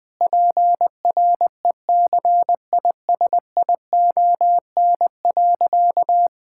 Morse code, heard as a keyed tone, about 20 wpm, 700 Hz. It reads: PRECISION.
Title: Difficult Sentences with Hints - 20wpm